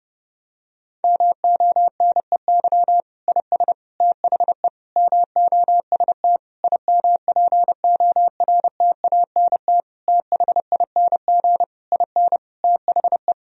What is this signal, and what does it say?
Morse code, heard as a keyed tone, about 30 wpm, 700 Hz. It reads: MONEY IS THE MOST IMPORTANT THING IN THE
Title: Sentences from Top 1000 Words - Review Part 1 - 30wpm